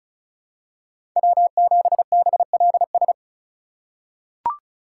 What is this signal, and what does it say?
Morse code, heard as a keyed tone, about 34 wpm, 700 Hz. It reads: W7BLS
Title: USA Call Signs - 5 Characters - 2x Character Spacing - 35wpm